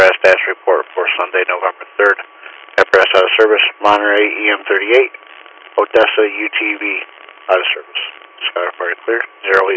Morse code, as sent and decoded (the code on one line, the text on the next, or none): none